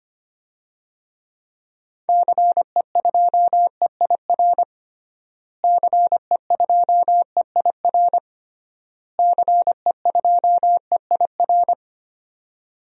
Morse code, heard as a keyed tone, about 25 wpm, 700 Hz. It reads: CE2EIR CE2EIR CE2EIR
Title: DXCC-100 Call Signs - 6 Characters - Repeated 3x - 25wpm